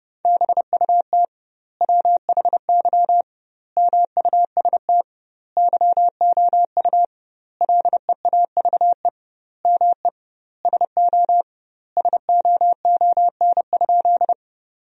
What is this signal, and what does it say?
Morse code, heard as a keyed tone, about 32 wpm, 700 Hz. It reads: BUT WHY MUST YOU LEAVE ME SO SOON?
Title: 2x Word-Spacing - Sentences from Top 400 Words - 30wpm